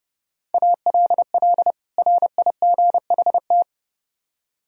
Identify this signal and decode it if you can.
Morse code, heard as a keyed tone, about 32 wpm, 700 Hz. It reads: ALL RIGHT